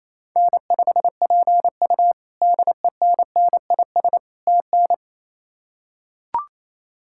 Morse code, heard as a keyed tone, about 27 wpm, 700 Hz. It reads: N5PU DENNIS TN
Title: K1USN SST QSO Practice - 28wpm